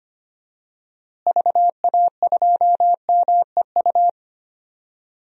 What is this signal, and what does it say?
Morse code, heard as a keyed tone, about 25 wpm, 700 Hz. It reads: VA2MEU
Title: DXCC-100 Call Signs - 6 Characters - Repeated 3x - 25wpm